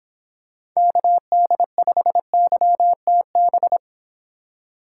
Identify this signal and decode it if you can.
Morse code, heard as a keyed tone, about 27 wpm, 700 Hz. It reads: KD5YTB
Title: Speed Racing - USA 2x3 Call Signs 22wpm